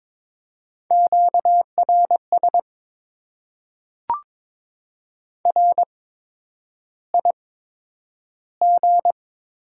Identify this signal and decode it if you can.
Morse code, heard as a keyed tone, about 22 wpm, spaced as for 9 wpm, 700 Hz. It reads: QRS RIG
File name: Top 90 QSO Elements - 8x Character Spacing - 22wpm